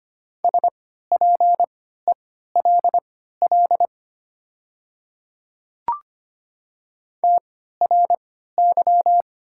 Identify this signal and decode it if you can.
Morse code, heard as a keyed tone, about 25 wpm, spaced as for 14 wpm, 700 Hz. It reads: SPELL TRY
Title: 3x Character Spacing - Top 500 Words - 25wpm